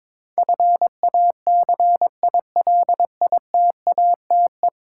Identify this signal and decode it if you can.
Morse code, heard as a keyed tone, about 22 wpm, 700 Hz. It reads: FACILITATE
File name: practice / 22wpm / 10 Letter Words - First 3 Letters Given - 22wpm